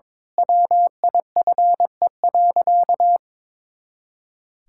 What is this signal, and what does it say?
Morse code, heard as a keyed tone, about 22 wpm, 700 Hz. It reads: WIFE.
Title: Pangram Challenge v2 - 22wpm